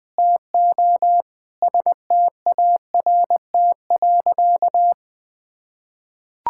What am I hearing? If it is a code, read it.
Morse code, heard as a keyed tone, about 20 wpm, 700 Hz. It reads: TO START.
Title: Sentences with 8-Word Phrases - Spoken Beginning - 20wpm